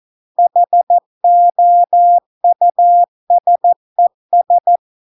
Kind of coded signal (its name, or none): Morse code